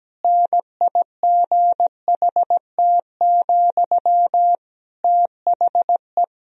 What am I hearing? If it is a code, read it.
Morse code, heard as a keyed tone, about 17 wpm, 700 Hz. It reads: NIGHT, THE
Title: Sentences from Top 1000 Words - Review Part 4 - 17wpm